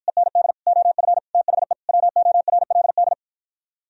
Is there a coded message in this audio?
Yes: Morse code from 0.1 to 3.1 s, heard as a keyed tone, about 52 wpm, 700 Hz. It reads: END OF THE WORLD